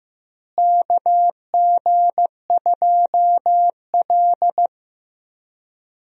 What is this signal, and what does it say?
Morse code, heard as a keyed tone, about 16 wpm, 700 Hz. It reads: KG2L
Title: USA Call Signs - 4 Characters - Repeated 2x - 15wpm